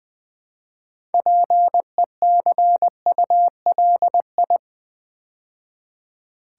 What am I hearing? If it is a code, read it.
Morse code, heard as a keyed tone, about 20 wpm, 700 Hz. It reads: PECULI
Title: Build A Word - 3 Letters At A Time - 8-Letter Words - 20wpm